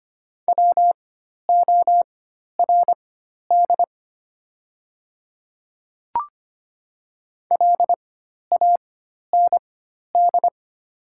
Morse code, heard as a keyed tone, about 25 wpm, spaced as for 12 wpm, 690 Hz. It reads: WORD LAND
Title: 4x Character Spacing - Top 200 Words - 25wpm